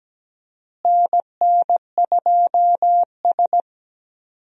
Morse code, heard as a keyed tone, about 17 wpm, 700 Hz. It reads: NN2S
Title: USA Call Signs - 4 Characters - Repeated 3x - 17wpm